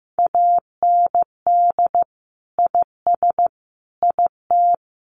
Morse code, heard as a keyed tone, about 15 wpm, 700 Hz. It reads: AND IS IT
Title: Sentences from Top 600 Words 15wpm